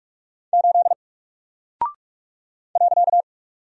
Morse code, heard as a keyed tone, about 44 wpm, 700 Hz. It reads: 8 .
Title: Single Character (Letter:Number:Punctuation:Prosign) 45wpm